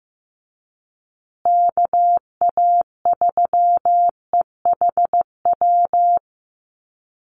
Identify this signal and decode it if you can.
Morse code, heard as a keyed tone, about 15 wpm, 700 Hz. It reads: KA3EHW
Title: USA Call Signs Repeated 3x 15wpm